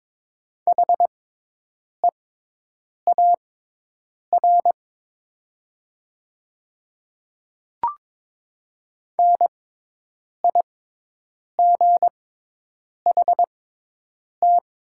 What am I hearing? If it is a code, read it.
Morse code, heard as a keyed tone, about 22 wpm, spaced as for 8 wpm, 700 Hz. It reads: HEAR NIGHT
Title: 6x Character Spacing - Top 500 Words - 22wpm